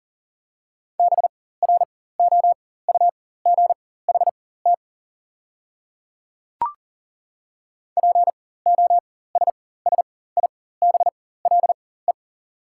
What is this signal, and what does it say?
Morse code, heard as a keyed tone, about 40 wpm, spaced as for 19 wpm, 690 Hz. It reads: BROUGHT POSSIBLE
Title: 4x Character Spacing - Top 500 Words - 40wpm